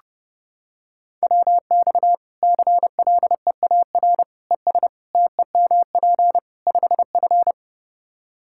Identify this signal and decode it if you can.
Morse code, heard as a keyed tone, about 32 wpm, 700 Hz. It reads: WX CLEAR ES TEMP 5F
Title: Practice QSO - Exchange 2 - Info - 30wpm